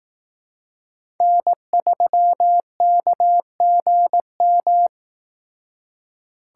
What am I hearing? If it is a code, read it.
Morse code, heard as a keyed tone, about 18 wpm, 700 Hz. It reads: N3KGM
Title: Speed Racing - USA 1x3 Call Signs 15wpm